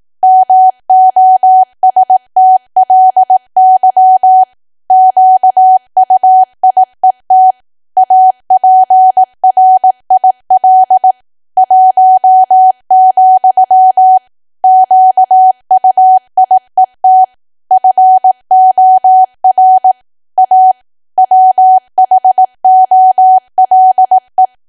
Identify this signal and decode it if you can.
Morse code, heard as a keyed tone, about 18 wpm, 750 Hz. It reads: MOSTLY QUIET APRIL 1, QUIET FOR A WHOLE
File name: cwa / W1AW-Level2 / 18WPM